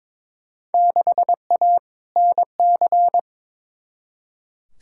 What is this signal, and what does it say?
Morse code, heard as a keyed tone, about 22 wpm, 700 Hz. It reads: 6A NC
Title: Field Day Exchanges Encore - 22wpm